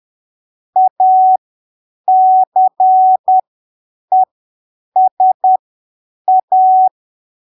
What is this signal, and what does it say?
Morse code, heard as a keyed tone, about 10 wpm, spaced as for 7 wpm, 750 Hz. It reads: ACESA
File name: arquivos / texto / texto 08 10ppm